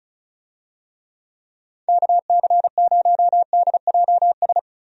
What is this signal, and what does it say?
Morse code, heard as a keyed tone, about 36 wpm, 700 Hz. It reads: KC0DJS
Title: USA Call Signs - 6 Characters - 35wpm